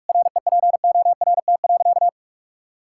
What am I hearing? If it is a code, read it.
Morse code, heard as a keyed tone, about 46 wpm, 700 Hz. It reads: REPORT.
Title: Sentences from Top 1000 Words - Review Part 6 - 45wpm